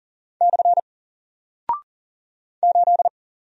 Morse code, heard as a keyed tone, about 39 wpm, 700 Hz. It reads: / 8
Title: Single Character (Letter:Number:Punctuation:Prosign) Alt 40wpm